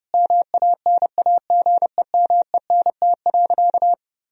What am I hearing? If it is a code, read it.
Morse code, heard as a keyed tone, about 30 wpm, 700 Hz. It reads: MANAGEMENT.
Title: Sentences from Top 1000 Words - Review Part 4 - 30wpm